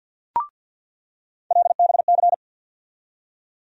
Morse code, heard as a keyed tone, about 48 wpm, 700 Hz. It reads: LBC
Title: USA Call Sign Postfixes - 3 Letters 50wpm